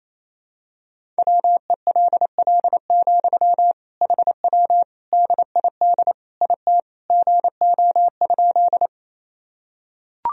Morse code, heard as a keyed tone, about 28 wpm, 700 Hz. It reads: WELL, HW DID IT GO?
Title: Sentences from Top 200 Words and Abbreviations 28wpm